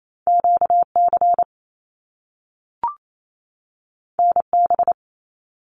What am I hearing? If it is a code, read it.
Morse code, heard as a keyed tone, about 28 wpm, 700 Hz. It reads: QC NB
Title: Canadian Provinces and Territories 28wpm